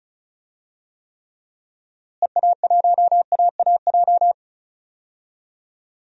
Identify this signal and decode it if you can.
Morse code, heard as a keyed tone, about 34 wpm, 700 Hz. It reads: EA1AAJ
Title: DXCC-100 Call Signs - 6 Characters - Repeated 2x - 35wpm